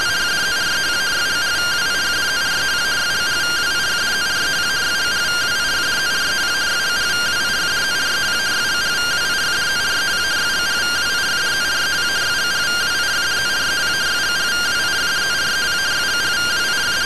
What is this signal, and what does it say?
Two-tone teleprinter signal, about 170 Hz shift.